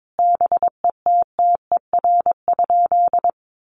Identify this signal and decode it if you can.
Morse code, heard as a keyed tone, about 22 wpm, 700 Hz. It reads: BETTER?
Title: Sentences from Top 400 Words 22wpm